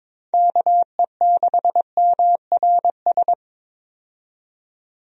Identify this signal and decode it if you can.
Morse code, heard as a keyed tone, about 22 wpm, 700 Hz. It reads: KE6MRS
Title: USA Call Signs - 6 Characters - 22wpm